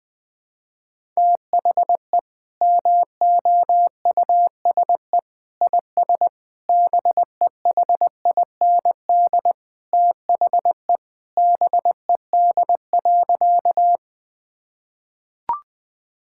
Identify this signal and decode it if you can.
Morse code, heard as a keyed tone, about 21 wpm, 700 Hz. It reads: THE MOUSE IS BEHIND THE BED.
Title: Sentences from Top 1000 Words - Review Part 5 - 20wpm